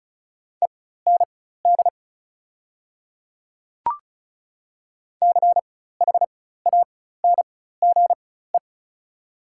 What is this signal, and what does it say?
Morse code, heard as a keyed tone, about 36 wpm, spaced as for 16 wpm, 690 Hz. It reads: END CHANGE
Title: 4x Character Spacing - Top 200 Words - 35wpm